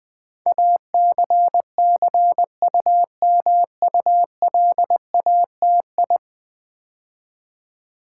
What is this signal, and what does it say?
Morse code, heard as a keyed tone, about 20 wpm, 700 Hz. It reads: ACCUMULATI
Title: Build A Word - 5 Letters At A Time - 12-Letter Words - 20wpm